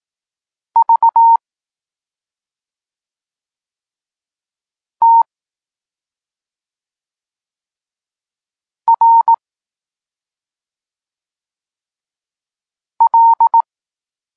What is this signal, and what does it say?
Morse code, heard as a keyed tone, about 18 wpm, 920 Hz. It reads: V T R L
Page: Tekst 32 – Doskonalenie odbioru poznanych liter